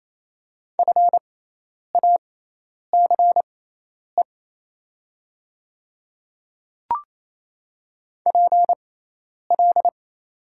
Morse code, heard as a keyed tone, about 28 wpm, spaced as for 10 wpm, 700 Hz. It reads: FACE PL
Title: 6x Character Spacing - Top 300 Words - 28wpm